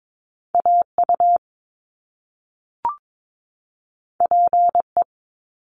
Morse code, heard as a keyed tone, about 22 wpm, 700 Hz. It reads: AU PE
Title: Top Two Letter Combinations in Words 22wpm